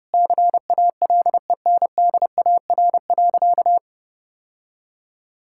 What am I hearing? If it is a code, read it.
Morse code, heard as a keyed tone, about 31 wpm, 700 Hz. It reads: CALENDAR.
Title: Sentences from Top 1600 Words 30wpm